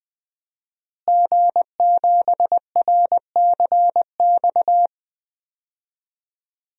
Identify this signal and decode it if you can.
Morse code, heard as a keyed tone, about 21 wpm, 700 Hz. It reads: G7RCX
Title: DXCC-100 Call Signs - 5 Characters - Repeated 3x - 20wpm